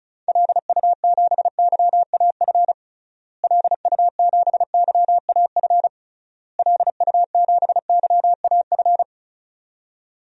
Morse code, heard as a keyed tone, about 34 wpm, 700 Hz. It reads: LU7YAF LU7YAF LU7YAF
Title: DXCC-100 Call Signs - 6 Characters - Repeated 3x - 35wpm